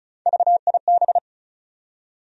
Morse code, heard as a keyed tone, about 36 wpm, 700 Hz. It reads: VIB
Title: USA Call Sign Postfixes - 3 Letters 35wpm